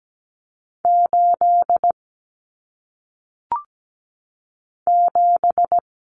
Morse code, heard as a keyed tone, about 17 wpm, 700 Hz. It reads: 8 7
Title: Common Cut Numbers - 1-Digit 17wpm